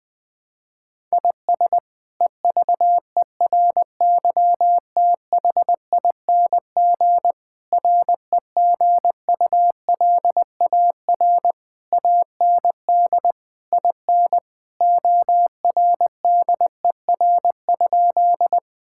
Morse code, heard as a keyed tone, about 20 wpm, 700 Hz. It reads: IS EVERYTHING REGULAR AND IN ORDER?
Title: Sentences from Top 1000 Words - Review Part 6 - 20wpm